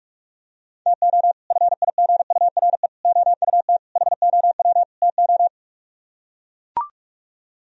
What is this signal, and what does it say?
Morse code, heard as a keyed tone, about 44 wpm, 700 Hz. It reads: TO FIGURE OUT HOW TO